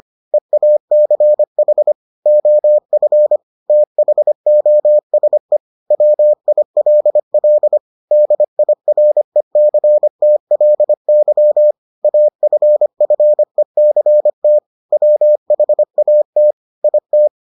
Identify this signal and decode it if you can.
Morse code, heard as a keyed tone, about 25 wpm, 600 Hz. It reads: EACH OF THOSE WILL DIRECTLY AFFECT WHAT IT